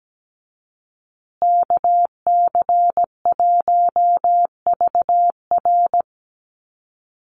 Morse code, heard as a keyed tone, about 17 wpm, 700 Hz. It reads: KC1VR
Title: USA Call Signs Repeated 3x 17wpm